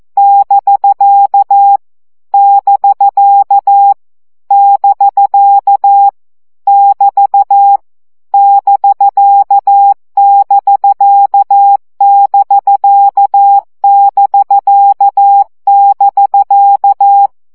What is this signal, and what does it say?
Morse code, heard as a keyed tone, about 14 wpm, 800 Hz. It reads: <BK> <BK> <BK> <BT> <BK><BK><BK><BK><BK>
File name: CW gyakorlas bk 12 baud